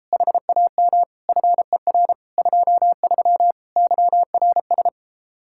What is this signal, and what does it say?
Morse code, heard as a keyed tone, about 33 wpm, 700 Hz. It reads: HAM FER 23 YRS